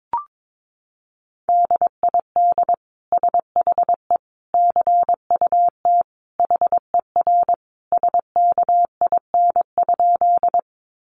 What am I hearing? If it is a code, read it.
Morse code, heard as a keyed tone, about 22 wpm, 700 Hz. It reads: DID SHE CUT HER SKIN?
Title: Sentences from Top 1300 Words 22wpm